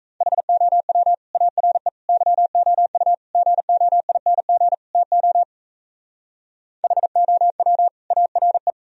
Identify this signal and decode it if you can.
Morse code, heard as a keyed tone, about 43 wpm, 700 Hz. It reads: HOW ARE YOU GOING TO HOW ARE